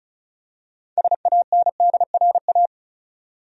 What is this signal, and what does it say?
Morse code, heard as a keyed tone, about 34 wpm, 700 Hz. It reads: SANDRA
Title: USA Common Names - 35wpm